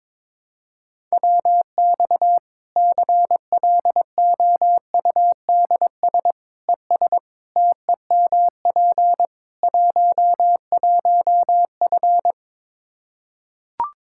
Morse code, heard as a keyed tone, about 22 wpm, 700 Hz. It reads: WX CLOUDS ES TEMP 11F
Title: Practice QSO - Exchange 2 - Info - 22wpm